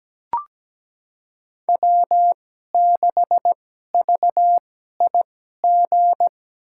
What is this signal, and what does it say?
Morse code, heard as a keyed tone, about 17 wpm, spaced as for 12 wpm, 700 Hz. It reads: W6VIG
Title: USA Call Signs - 5 Characters - 2x Character Spacing - 17wpm